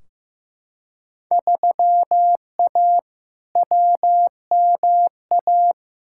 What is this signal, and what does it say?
Morse code, heard as a keyed tone, about 15 wpm, 700 Hz. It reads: 3A WMA
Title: Field Day Exchanges - 15wpm